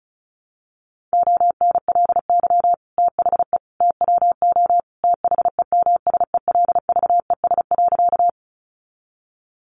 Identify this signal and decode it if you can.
Morse code, heard as a keyed tone, about 34 wpm, 700 Hz. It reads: ONLY THE TWO THEMSELVES.
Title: Sentences from Top 800 Words 35wpm